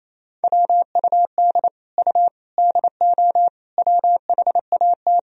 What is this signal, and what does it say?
Morse code, heard as a keyed tone, about 28 wpm, 700 Hz. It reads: WUD U DO WHAT
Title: Sentences from Top 100 Words and Abbreviations 28wpm